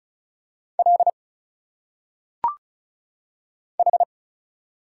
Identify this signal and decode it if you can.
Morse code, heard as a keyed tone, about 34 wpm, 700 Hz. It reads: L H
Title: Single Letters - Phonetic Pronunciation 35wpm